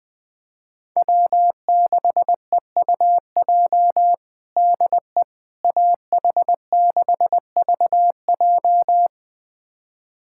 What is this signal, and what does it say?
Morse code, heard as a keyed tone, about 20 wpm, 700 Hz. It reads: W6EUJ DE AH6VJ
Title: USA Callsign DE Callsign - 5 Characters - 20wpm